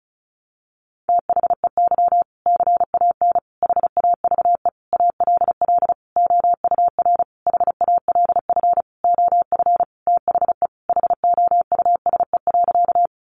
Morse code, heard as a keyed tone, about 34 wpm, 700 Hz. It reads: THEY CAN HAVE ALL OUR HALF OF THE HOUSE.
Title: Sentences from Top 400 Words 35wpm